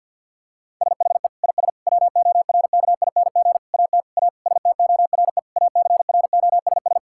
Morse code, heard as a keyed tone, about 50 wpm, 700 Hz. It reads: SHE IS WORKING AT A STORE ACROSS
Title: Sentences from Top 1000 Words - Review Part 5 - 50wpm